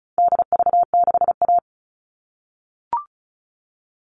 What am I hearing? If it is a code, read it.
Morse code, heard as a keyed tone, about 34 wpm, 700 Hz. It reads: DV6A
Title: All Cut Numbers - 4-Digit 35wpm